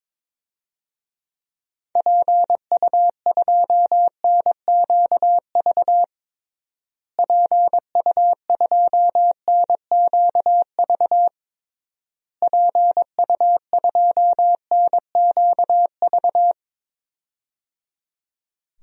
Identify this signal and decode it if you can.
Morse code, heard as a keyed tone, about 21 wpm, 700 Hz. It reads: PU2NQV PU2NQV PU2NQV
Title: DXCC-100 Call Signs - 6 Characters - Repeated 3x - 22wpm